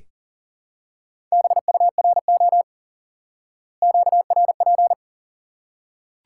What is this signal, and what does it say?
Morse code, heard as a keyed tone, about 40 wpm, 700 Hz. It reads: BURO QRP
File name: IQE Course 3 - Lesson 89 - YAGI - Two Elements - 40wpm